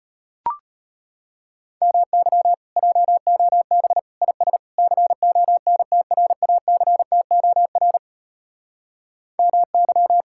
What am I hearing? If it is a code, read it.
Morse code, heard as a keyed tone, about 33 wpm, 700 Hz. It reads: MY JOB IS CONTRACTOR MY